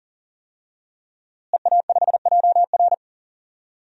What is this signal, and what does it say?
Morse code, heard as a keyed tone, about 40 wpm, 700 Hz. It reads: EA5JR
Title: DXCC-100 Call Signs - 5 Characters - Repeated 2x - 40wpm